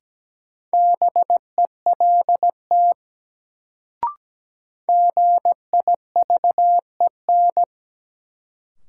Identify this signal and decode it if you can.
Morse code, heard as a keyed tone, about 17 wpm, 700 Hz. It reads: BELT GIVEN
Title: Sentences from Top 1400 Words - New Word Drill - 17wpm